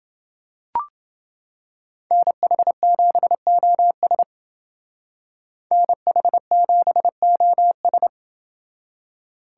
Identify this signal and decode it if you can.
Morse code, heard as a keyed tone, about 29 wpm, 700 Hz. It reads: NH7OS NH7OS